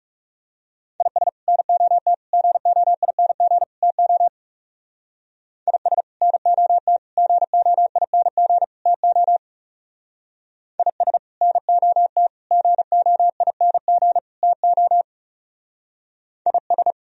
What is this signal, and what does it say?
Morse code, heard as a keyed tone, about 35 wpm, 690 Hz. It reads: IS NOT GOING TO IS NOT GOING TO IS NOT GOING TO IS
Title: Speed-Racing - Sets of 4 Words 30wpm